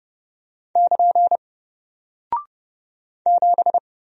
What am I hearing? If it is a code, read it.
Morse code, heard as a keyed tone, about 29 wpm, 700 Hz. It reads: <KN> 7
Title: Single Character (Letter:Number:Punctuation:Prosign) 30wpm